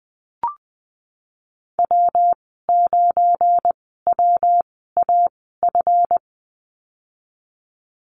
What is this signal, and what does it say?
Morse code, heard as a keyed tone, about 20 wpm, spaced as for 15 wpm, 700 Hz. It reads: W9WAF
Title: 186 Lesson 30g - Letter b - Call Signs 20wpm